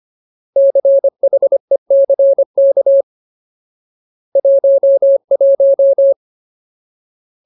Morse code, heard as a keyed tone, about 25 wpm, 550 Hz. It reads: CHECK 11
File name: Ultimate Word List / 25WPM Ultimate